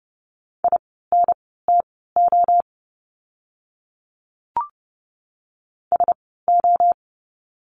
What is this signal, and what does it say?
Morse code, heard as a keyed tone, about 30 wpm, spaced as for 17 wpm, 700 Hz. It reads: INTO SO
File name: course / 30:10wpm / 230 Lesson 37f - Letter x - Words 30wpm